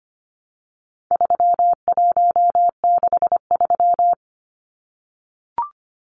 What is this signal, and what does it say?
Morse code, heard as a keyed tone, about 25 wpm, 700 Hz. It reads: 3163
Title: Numbers - 4-Digit 25wpm